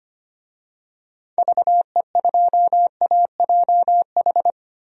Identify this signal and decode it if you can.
Morse code, heard as a keyed tone, about 25 wpm, 700 Hz. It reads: VE2AJH
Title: DXCC-100 Call Signs - 6 Characters - Repeated 2x - 25wpm